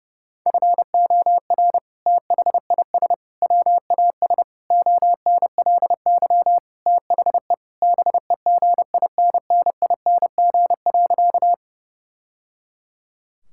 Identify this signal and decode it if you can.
Morse code, heard as a keyed tone, about 30 wpm, 700 Hz. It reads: FOR THIS WAS ONLY THE BEGINNING.
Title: Sentences from Top 1000 Words - Review Part 5 - 30wpm